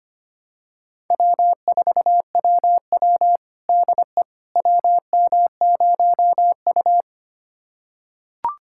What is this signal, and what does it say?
Morse code, heard as a keyed tone, about 25 wpm, 700 Hz. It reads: W4WW DE WM0U
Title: USA Callsign DE Callsign - 4 Characters - 25wpm